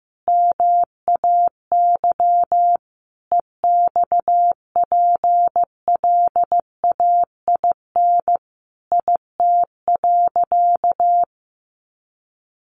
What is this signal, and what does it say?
Morse code, heard as a keyed tone, about 15 wpm, 700 Hz. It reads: MAY EXPLAIN IT.
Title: Sentences from Top 1100 Words 15wpm